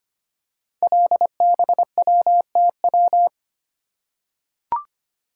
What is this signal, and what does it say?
Morse code, heard as a keyed tone, about 25 wpm, 700 Hz. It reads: LBWTW
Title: Code Groups - 5 Letter:Numbers - 25wpm